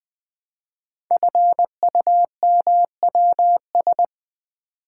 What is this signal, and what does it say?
Morse code, heard as a keyed tone, about 20 wpm, 700 Hz. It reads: FUMWS
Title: Code Groups - 5 Letters - 20wpm